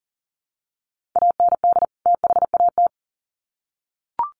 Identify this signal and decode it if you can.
Morse code, heard as a keyed tone, about 40 wpm, 700 Hz. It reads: AND THAT
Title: Sets of 2 Words 40wpm